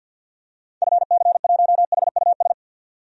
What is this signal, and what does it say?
Morse code, heard as a keyed tone, about 48 wpm, 700 Hz. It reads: FC1HUS
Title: DXCC-100 Call Signs - 6 Characters - 50wpm